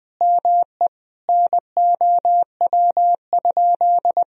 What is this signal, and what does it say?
Morse code, heard as a keyed tone, about 21 wpm, 700 Hz. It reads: ME NOW?
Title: Sentences from Top 400 Words and Abbreviations 20wpm